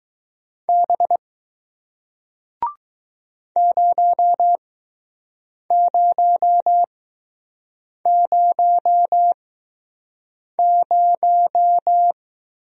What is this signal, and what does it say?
Morse code, heard as a keyed tone, about 21 wpm, spaced as for 6 wpm, 700 Hz. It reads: B 0000